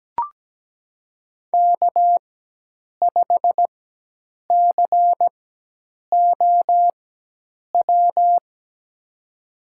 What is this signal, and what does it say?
Morse code, heard as a keyed tone, about 17 wpm, spaced as for 8 wpm, 700 Hz. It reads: K5COW
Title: USA Call Signs - 5 Characters - 4x Character Spacing - 17wpm